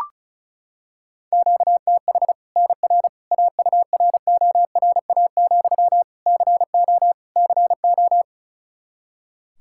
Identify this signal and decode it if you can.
Morse code, heard as a keyed tone, about 36 wpm, 700 Hz. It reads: QTH NR AURORA, CO CO
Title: Practice QSO - Exchange 1 - Beginning - 35wpm